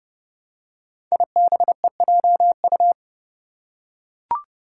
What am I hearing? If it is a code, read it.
Morse code, heard as a keyed tone, about 28 wpm, 700 Hz. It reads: IBEJU
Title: Five Challenging Letter-Number Combinations 30wpm